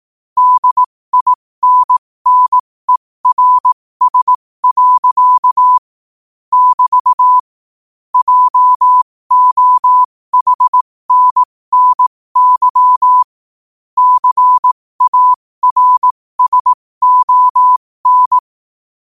Morse code, heard as a keyed tone, about 18 wpm, spaced as for 15 wpm, 1000 Hz. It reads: DINNERS. <BT> JOHNNY CARSON
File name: qotdfiles / 15wpm